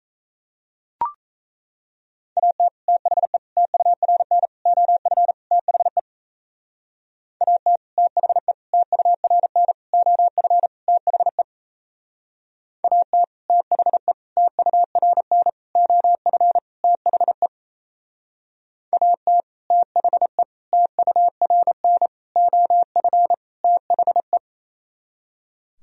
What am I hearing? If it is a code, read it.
Morse code, heard as a keyed tone, about 36 wpm, 700 Hz. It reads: AT THE TURN OF THE AT THE TURN OF THE AT THE TURN OF THE AT THE TURN OF THE